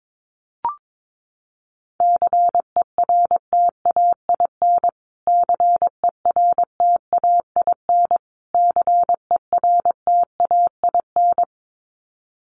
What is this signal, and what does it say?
Morse code, heard as a keyed tone, about 22 wpm, 700 Hz. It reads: CERTAIN CERTAIN CERTAIN
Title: Top 500 Words Repeated 22wpm